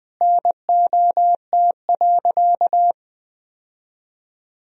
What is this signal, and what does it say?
Morse code, heard as a keyed tone, about 20 wpm, 700 Hz. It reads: NOT.
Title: Sentences from Top 1000 Words - Review Part 4 - 20wpm